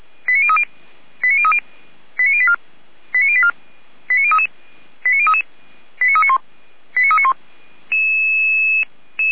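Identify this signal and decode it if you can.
Five-tone selective call (ZVEI), heard as quick runs of five tones, tones of about 80 ms. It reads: 89030 89030 89084 89084 89033 89033 89391 89391
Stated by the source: ZVEI five-tone selective call, code 89042, received with an SDR.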